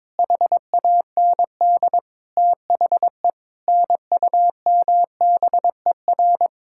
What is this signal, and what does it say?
Morse code, heard as a keyed tone, about 22 wpm, 700 Hz. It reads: HAND THE NUMBER
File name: practice / 22wpm / Sets of 8 Words - 22wpm